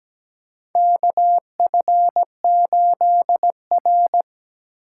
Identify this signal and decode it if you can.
Morse code, heard as a keyed tone, about 17 wpm, 700 Hz. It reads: KF8R
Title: USA Call Signs - 4x Character Spacing - 17wpm